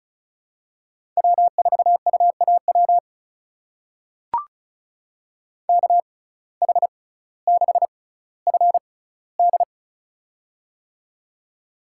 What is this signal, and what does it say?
Morse code, heard as a keyed tone, about 36 wpm, spaced as for 18 wpm, 700 Hz. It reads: W4UAW KH6FD
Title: USA Call Signs - 5 Characters - 6x Character Spacing - 35wpm